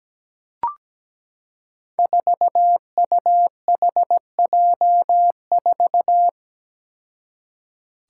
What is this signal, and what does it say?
Morse code, heard as a keyed tone, about 17 wpm, 700 Hz. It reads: 4UHJ4
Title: Five Challenging Letter-Number Combinations 17wpm